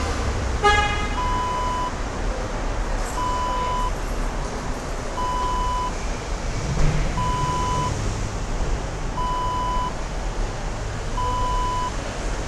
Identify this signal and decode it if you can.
There is no code.